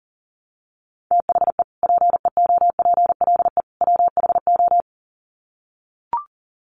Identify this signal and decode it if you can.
Morse code, heard as a keyed tone, about 40 wpm, 700 Hz. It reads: THE PEOPLE WHO
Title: Sets of 3 Words 40wpm